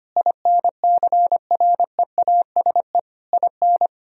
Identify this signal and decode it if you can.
Morse code, heard as a keyed tone, about 25 wpm, 700 Hz. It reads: INCREASE IN
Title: Sentences from Top 1000 Words - Review Part 4 - 25wpm